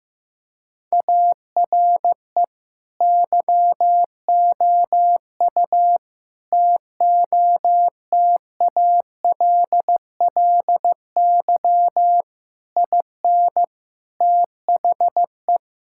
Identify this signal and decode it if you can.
Morse code, heard as a keyed tone, about 15 wpm, 700 Hz. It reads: ARE YOU TOTALLY IN THE
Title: Sentences from Top 1600 Words 15wpm